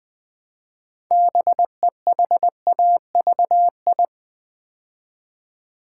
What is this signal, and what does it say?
Morse code, heard as a keyed tone, about 20 wpm, 700 Hz. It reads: BEHAVI
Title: Build A Word - 3 Letters At A Time - 9-Letter Words - 20wpm